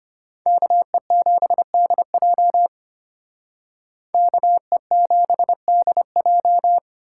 Morse code, heard as a keyed tone, about 33 wpm, 700 Hz. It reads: KE7DJ KE7DJ